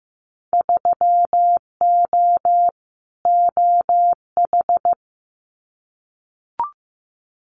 Morse code, heard as a keyed tone, about 15 wpm, 700 Hz. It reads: 3O OH
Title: Winter Field Day Exchanges 15wpm